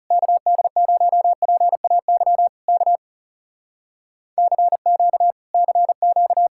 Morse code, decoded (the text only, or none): KD0PAY K CQ CQ